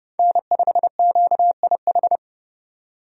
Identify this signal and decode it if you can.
Morse code, heard as a keyed tone, about 31 wpm, 700 Hz. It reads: N5QIH
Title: USA Call Signs - 5 Characters - 30wpm